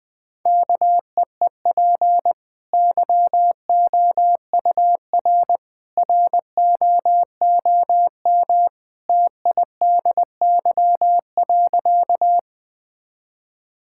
Morse code, heard as a keyed tone, about 21 wpm, 700 Hz. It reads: KEEP YOUR ROOM TIDY.
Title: Easy Sentences with Hints - 20wpm